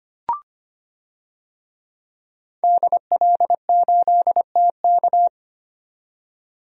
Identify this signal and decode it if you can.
Morse code, heard as a keyed tone, about 25 wpm, 700 Hz. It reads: DL8TK
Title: DXCC-100 Call Signs - 5 Characters - 25wpm